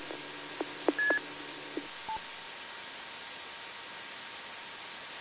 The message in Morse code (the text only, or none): none